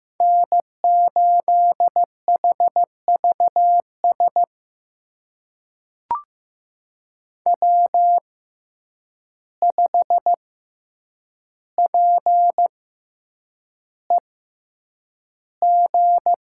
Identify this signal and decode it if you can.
Morse code, heard as a keyed tone, about 16 wpm, spaced as for 8 wpm, 700 Hz. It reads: N8HVS W5PEG
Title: USA Call Signs - 5 Characters - 6x Character Spacing - 15wpm